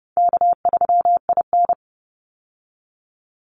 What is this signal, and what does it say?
Morse code, heard as a keyed tone, about 30 wpm, 700 Hz. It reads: K3IN
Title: Easy Call Signs Repeated 3x - 30wpm